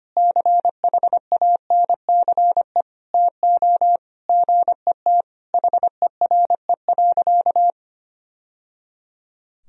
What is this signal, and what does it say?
Morse code, heard as a keyed tone, about 25 wpm, 700 Hz. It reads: CHANCE TO GET HERE.
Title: Sentences from Top 1000 Words - Review Part 2 - 25wpm